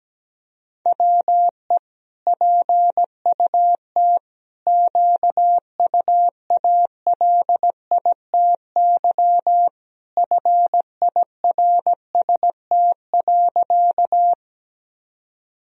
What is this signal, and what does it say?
Morse code, heard as a keyed tone, about 17 wpm, 700 Hz. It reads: WE PUT QUALITY FIRST.
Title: Sentences from Top 1000 Words - Review Part 4 - 17wpm